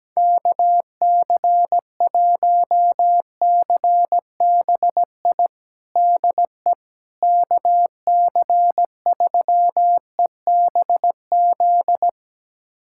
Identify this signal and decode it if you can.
Morse code, heard as a keyed tone, about 17 wpm, 700 Hz. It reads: KC1CBI DE KC3EBZ
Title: USA Callsign DE Callsign - 6 Characters - 17wpm